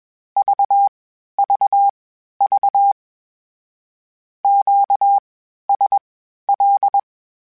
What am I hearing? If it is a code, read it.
Morse code, heard as a keyed tone, about 21 wpm, spaced as for 12 wpm, 800 Hz. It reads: VVV QSL
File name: (spaced) / Q-Calls Spaced-mp3s / Q Calls sp Once @20